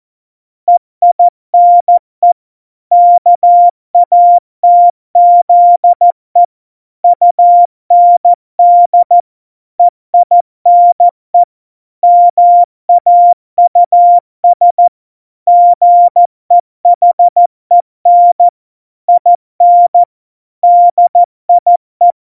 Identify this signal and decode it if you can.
Morse code, heard as a keyed tone, about 14 wpm, 700 Hz. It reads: EINE KATZE UND EINE MAUS GEHEN IN DIE
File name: Morsekurs / Witze / Witz 03 April 2018, 70 F80